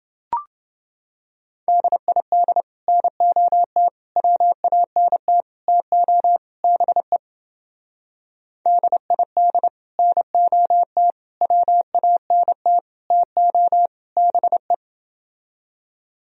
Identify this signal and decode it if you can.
Morse code, heard as a keyed tone, about 28 wpm, 700 Hz. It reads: DID NOT WANT TO BE DID NOT WANT TO BE